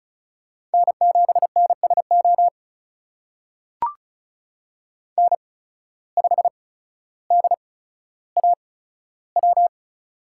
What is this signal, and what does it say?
Morse code, heard as a keyed tone, about 34 wpm, spaced as for 15 wpm, 700 Hz. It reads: N7NSO N5DAW
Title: USA Call Signs - 5 Characters - 8x Character Spacing - 35wpm